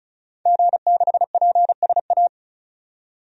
Morse code, heard as a keyed tone, about 36 wpm, 700 Hz. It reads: G6PSA
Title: DXCC-100 Call Signs - 5 Characters - 35wpm